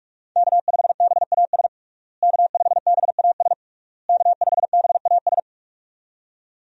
Morse code, heard as a keyed tone, about 46 wpm, 700 Hz. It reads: K5BAS K5BAS K5BAS
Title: USA Call Signs - 5 Characters - Repeated 3x - 45wpm